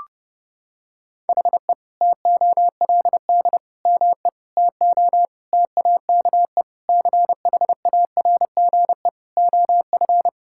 Morse code, heard as a keyed tone, about 30 wpm, 700 Hz. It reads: HE TOLD ME TO TAKE CHARGE OF
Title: Sentences from Top 1000 Words - Review Part 6 - 30wpm